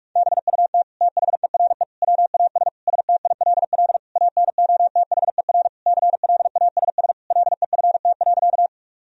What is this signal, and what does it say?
Morse code, heard as a keyed tone, about 46 wpm, 700 Hz. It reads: BUT THERE WAS STILL ANOTHER CLASS LEFT.
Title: Sentences from Top 1000 Words - Review Part 2 - 45wpm